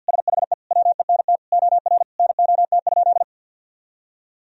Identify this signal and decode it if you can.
Morse code, heard as a keyed tone, about 50 wpm, 700 Hz. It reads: SHE WENT OR NOT?
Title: Sentences from Top 1000 Words - Review Part 3 - 50wpm